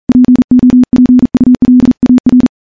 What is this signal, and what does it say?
Morse code, heard as a keyed tone, about 39 wpm, 250 Hz. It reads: POPULAR
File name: morse-training-40WPM